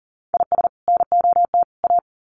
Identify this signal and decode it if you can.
Morse code, heard as a keyed tone, about 40 wpm, 700 Hz. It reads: IS NOT A
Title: Sets of 3 Words 40wpm